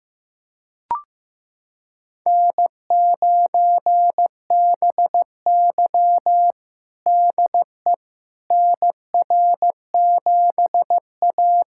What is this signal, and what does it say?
Morse code, heard as a keyed tone, about 15 wpm, 700 Hz. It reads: N9BY DE NR7A
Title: USA Callsign DE Callsign - 4 Characters - 15wpm